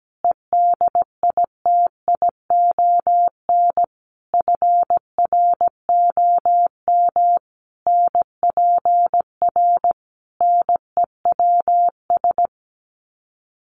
Morse code, heard as a keyed tone, about 17 wpm, 700 Hz. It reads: EDITION FROM NPR NEWS
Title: Sets of 5 Words Encore 17wpm